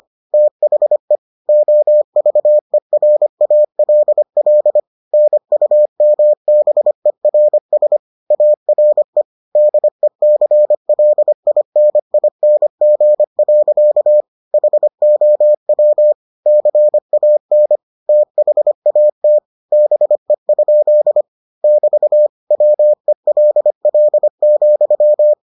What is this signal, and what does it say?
Morse code, heard as a keyed tone, about 25 wpm, 600 Hz. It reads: THE OVERALL NUMBERS ARE DECLINING. HOW CAN THAT BE? <BT> WELL,